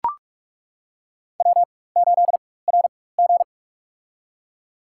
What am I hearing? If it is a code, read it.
Morse code, heard as a keyed tone, about 41 wpm, spaced as for 20 wpm, 700 Hz. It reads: W8RG
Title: USA Call Signs - 4 Characters - 4x Character Spacing - 45wpm